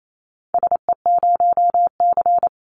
Morse code, heard as a keyed tone, about 28 wpm, 700 Hz. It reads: SE0C
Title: Call Signs - 28wpm